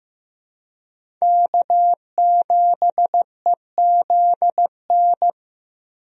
Morse code, heard as a keyed tone, about 15 wpm, 700 Hz. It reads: K7EZN
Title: USA Call Signs - 5 Characters - Repeated 2x - 15wpm